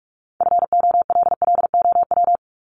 Morse code, heard as a keyed tone, about 44 wpm, 700 Hz. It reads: FOLLOW
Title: ES:S Suffix - Top 100 Words with Base Words 45wpm